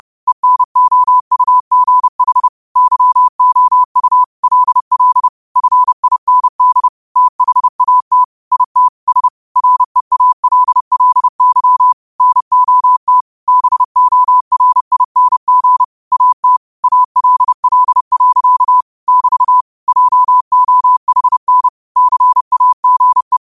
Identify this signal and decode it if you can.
Morse code, heard as a keyed tone, about 30 wpm, 1000 Hz. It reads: ENOUGH YOU LL FIND THAT IT S REALLY NOT BORING AT ALL. <BT> JOHN CAGE
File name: qotdfiles / 30wpm